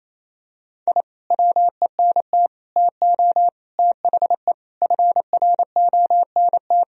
Morse code, heard as a keyed tone, about 28 wpm, 700 Hz. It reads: I WENT TO THE FRONT
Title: Sentences from Top 1000 Words - Review Part 2 - 28wpm